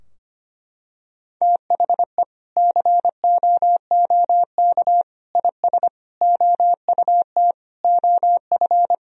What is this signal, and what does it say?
Morse code, heard as a keyed tone, about 25 wpm, 700 Hz. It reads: THE COOK IS OUT OF
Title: Sentences from Top 1000 Words - Review Part 4 - 25wpm